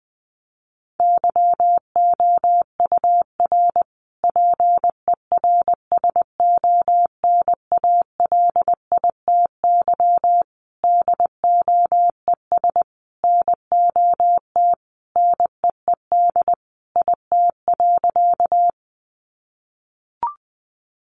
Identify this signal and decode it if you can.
Morse code, heard as a keyed tone, about 20 wpm, 700 Hz. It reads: YOUR PERSONALITY DOES NOT NEED IT.
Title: Sentences from Top 1400 Words 20wpm